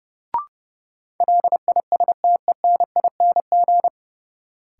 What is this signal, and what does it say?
Morse code, heard as a keyed tone, about 30 wpm, 700 Hz. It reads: LISTENING
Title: Sets of 5 Words - New Word Drill - 30wpm